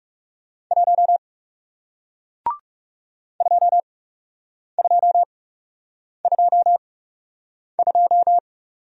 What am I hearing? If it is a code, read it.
Morse code, heard as a keyed tone, about 39 wpm, spaced as for 8 wpm, 700 Hz. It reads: 1 2222